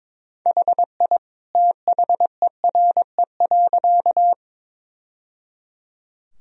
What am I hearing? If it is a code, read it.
Morse code, heard as a keyed tone, about 22 wpm, 700 Hz. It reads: HI THERE.
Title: Sentences from Top 1000 Words - Review Part 3 - 22wpm